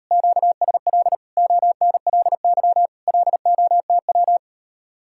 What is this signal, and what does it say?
Morse code, heard as a keyed tone, about 34 wpm, 700 Hz. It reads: QSL ONLY LOTW
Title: Speed-Racing Common QSO Phrases Encore - 28wpm